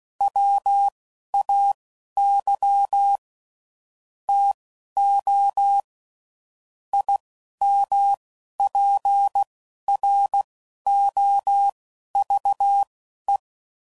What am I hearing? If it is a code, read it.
Morse code, heard as a keyed tone, about 16 wpm, spaced as for 11 wpm, 800 Hz. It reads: WAY TO IMPROVE